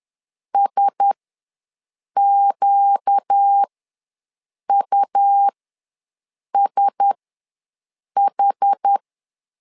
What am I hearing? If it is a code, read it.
Morse code, heard as a keyed tone, about 10 wpm, spaced as for 6 wpm, 800 Hz. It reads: SQUSH
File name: LEZIONI / 4 Q1S ws 10wpm 2208